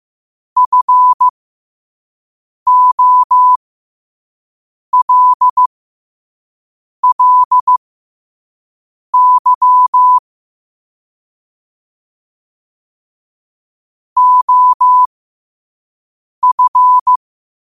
Morse code, heard as a keyed tone, about 15 wpm, spaced as for 5 wpm, 1000 Hz. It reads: FOLLY OF
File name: qotdfiles / 5wpm